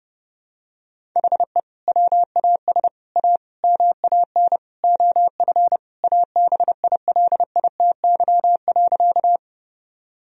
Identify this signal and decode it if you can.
Morse code, heard as a keyed tone, about 30 wpm, 700 Hz. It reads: HE WAS A MAN OF ABILITY.
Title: Sentences from Top 1000 Words - Review Part 3 - 30wpm